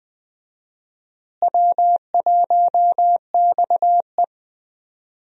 Morse code, heard as a keyed tone, about 20 wpm, 700 Hz. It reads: W1XE
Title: USA Call Signs - 4 Characters - Repeated 3x - 20wpm